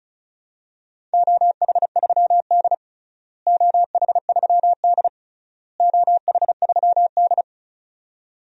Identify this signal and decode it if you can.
Morse code, heard as a keyed tone, about 36 wpm, 700 Hz. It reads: OH3D OH3D OH3D
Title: DXCC-100 Call Signs - 4 Characters - Repeated 3x - 35wpm